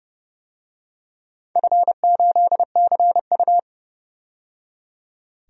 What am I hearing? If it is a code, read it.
Morse code, heard as a keyed tone, about 32 wpm, 700 Hz. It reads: F8CU
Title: DXCC-100 Call Signs - 4 Characters - 30wpm